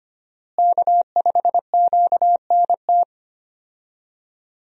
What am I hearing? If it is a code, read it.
Morse code, heard as a keyed tone, about 25 wpm, 700 Hz. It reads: K5QNT
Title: USA Call Signs - 5 Characters - Repeated 2x - 25wpm